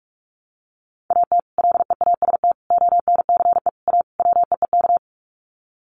Morse code, heard as a keyed tone, about 44 wpm, 690 Hz. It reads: AT LEAST ONCE A WEEK
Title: Sets of 5 Words Encore2 45wpm